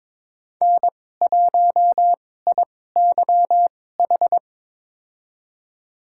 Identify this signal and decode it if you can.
Morse code, heard as a keyed tone, about 22 wpm, spaced as for 16 wpm, 700 Hz. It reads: N1IYH
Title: USA Call Signs - 5 Characters - 2x Character Spacing - 22wpm